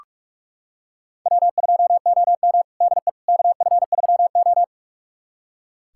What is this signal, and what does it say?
Morse code, heard as a keyed tone, about 44 wpm, 700 Hz. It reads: W2OM DE KF3O
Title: USA Callsign DE Callsign - 4 Characters - 45wpm